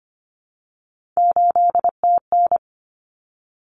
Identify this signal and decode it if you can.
Morse code, heard as a keyed tone, about 25 wpm, 700 Hz. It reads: 8TN
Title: Common Cut Numbers - 3-Digit 25wpm